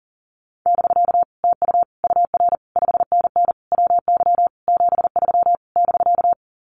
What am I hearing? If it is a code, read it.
Morse code, heard as a keyed tone, about 40 wpm, 700 Hz. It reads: <BK> TU UR 5NN WY 73 <BK>
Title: Common QSO Phrases Encore 40wpm